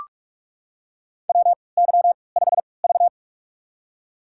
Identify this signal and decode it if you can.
Morse code, heard as a keyed tone, about 44 wpm, spaced as for 25 wpm, 700 Hz. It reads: WY5V
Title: USA Call Signs - 4 Characters - 3x Character Spacing - 45wpm